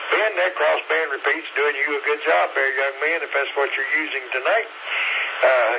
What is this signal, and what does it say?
Emergency Alert System (SAME) tones (header bursts), heard as harsh buzzing bursts, with speech. It is heard from 1.1 to 5.3 s.